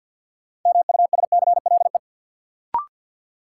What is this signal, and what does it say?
Morse code, heard as a keyed tone, about 53 wpm, 700 Hz. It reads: MUSCLE
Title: Sentences from Top 1300 Words - New Word Drill Alt - 50wpm